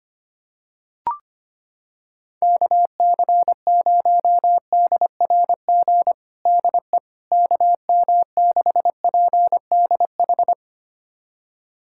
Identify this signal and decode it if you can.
Morse code, heard as a keyed tone, about 25 wpm, 700 Hz. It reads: KC0DRG DE KM6PDH
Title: USA Callsign DE Callsign - 6 Characters - 25wpm